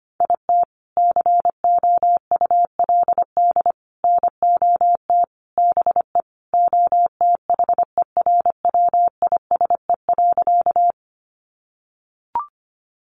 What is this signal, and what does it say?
Morse code, heard as a keyed tone, about 25 wpm, 700 Hz. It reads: IT COULD NOT BE OTHERWISE.
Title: Sentences from Top 1200 Words 25wpm